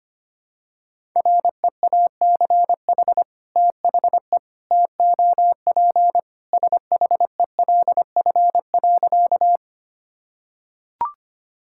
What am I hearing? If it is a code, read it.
Morse code, heard as a keyed tone, about 25 wpm, 700 Hz. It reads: REACH THE TOP SHELF.
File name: practice / 25wpm / Sentences with 2-Word Phrases - 25wpm